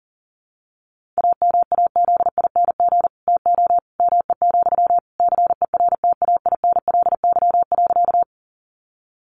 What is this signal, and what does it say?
Morse code, heard as a keyed tone, about 40 wpm, 700 Hz. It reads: AMAZING TO ME, CERTAINLY.
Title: Sentences from Top 1500 Words 40wpm